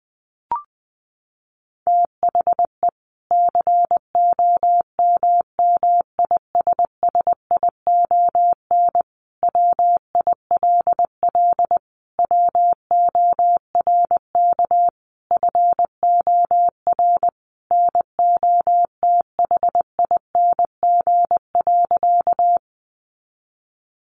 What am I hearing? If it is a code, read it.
Morse code, heard as a keyed tone, about 20 wpm, 700 Hz. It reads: THE COMMISSION WILL WORK FOR NOTHING.
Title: Sentences from Top 1400 Words 20wpm